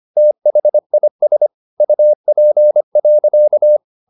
Morse code, heard as a keyed tone, about 25 wpm, 600 Hz. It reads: THIS UP.
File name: the old man and the sea part 1 25wpm